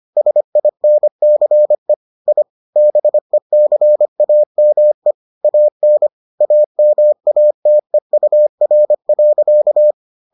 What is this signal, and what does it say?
Morse code, heard as a keyed tone, about 25 wpm, 600 Hz. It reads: SINCE I BECAME AN AMATEUR.